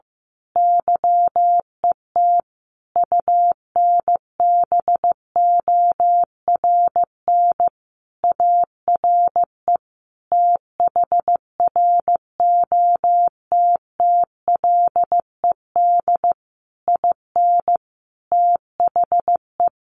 Morse code, heard as a keyed tone, about 15 wpm, 700 Hz. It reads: YET UNBORN ARE THROTTLED IN THE
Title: A Sound Of Thunder 15wpm